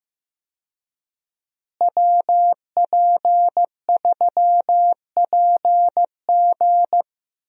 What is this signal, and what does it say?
Morse code, heard as a keyed tone, about 15 wpm, 700 Hz. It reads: WP3PG
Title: USA Call Signs - 5 Characters - Repeated 3x - 15wpm